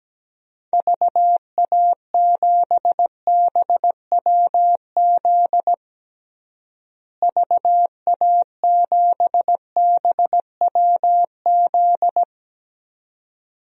Morse code, heard as a keyed tone, about 17 wpm, 700 Hz. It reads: VA7BWZ VA7BWZ
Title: DXCC-100 Call Signs - 6 Characters - Repeated 2x - 17wpm